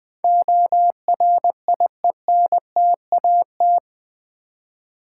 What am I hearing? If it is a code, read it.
Morse code, heard as a keyed tone, about 21 wpm, 700 Hz. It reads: ORIENTAT
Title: Build A Word - 4 Letters At A Time - 11-Letter Words - 20wpm